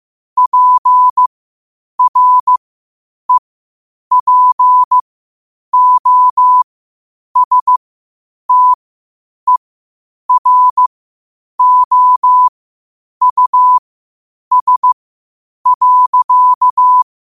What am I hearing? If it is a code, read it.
Morse code, heard as a keyed tone, about 15 wpm, spaced as for 8 wpm, 1000 Hz. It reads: PREPOSTEROUS.
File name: qotdfiles / 8wpm